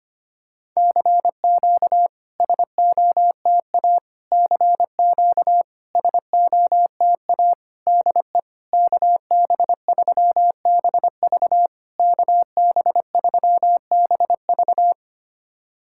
Morse code, heard as a keyed tone, about 25 wpm, 690 Hz. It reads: CQ SOTA CQ SOTA DE KB3BV KB3BV
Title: SOTA QSO Practice 25wpm